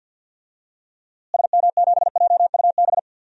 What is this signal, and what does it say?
Morse code, heard as a keyed tone, about 52 wpm, 700 Hz. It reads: SM7JUB
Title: DXCC-100 Call Signs - 6 Characters - Repeated 2x - 50wpm